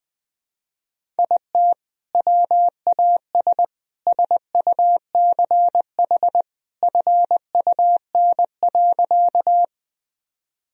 Morse code, heard as a keyed tone, about 21 wpm, 700 Hz. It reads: IT WAS SUCH FUN.
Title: Sentences from Top 1000 Words - Review Part 4 - 20wpm